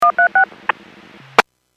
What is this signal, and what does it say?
DTMF, heard as telephone touch tones, tones of about 110 ms. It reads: 136
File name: WPS / 02633 16-47 WPS 451.25 WPS